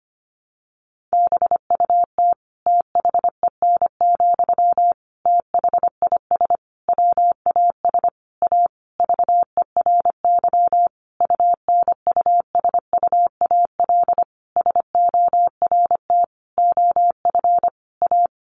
Morse code, heard as a keyed tone, about 25 wpm, 700 Hz. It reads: BUT THEN, THIS WAS A VERY UNUSUAL SORT OF A
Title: Sentences from Top 1400 Words 25wpm